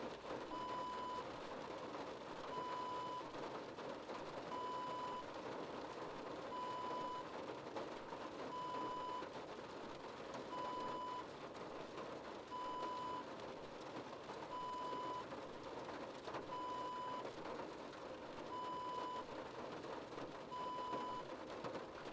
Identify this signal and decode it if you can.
There is no code.